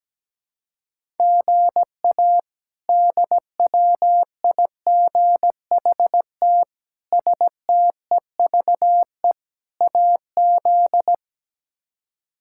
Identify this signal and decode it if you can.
Morse code, heard as a keyed tone, about 17 wpm, 700 Hz. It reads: GA DWIGHT STEVE AZ
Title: K1USN SST QSO Practice - 17wpm